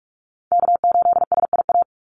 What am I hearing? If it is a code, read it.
Morse code, heard as a keyed tone, about 44 wpm, 700 Hz. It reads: K8SIA
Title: Call Signs Repeated 2x - 45wpm